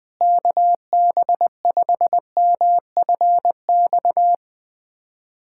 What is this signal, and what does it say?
Morse code, heard as a keyed tone, about 20 wpm, 700 Hz. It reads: KB5MFX
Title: USA Callsign DE Callsign - 6 Characters - 20wpm